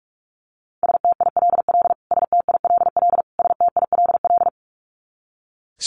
Morse code, heard as a keyed tone, about 44 wpm, 700 Hz. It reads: STILL STILL STILL
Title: Top 500 Words Repeated 45wpm